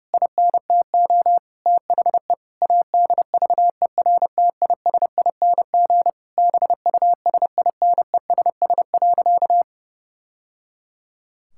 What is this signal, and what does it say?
Morse code, heard as a keyed tone, about 30 wpm, 700 Hz. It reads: INTO THE ADVERTISING BUSINESS.
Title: Sentences from Top 1000 Words - Review Part 6 - 30wpm